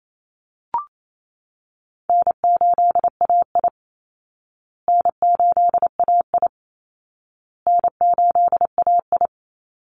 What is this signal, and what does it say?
Morse code, heard as a keyed tone, about 28 wpm, 700 Hz. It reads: N8AI N8AI N8AI
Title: Call Signs Repeated 2x - 28wpm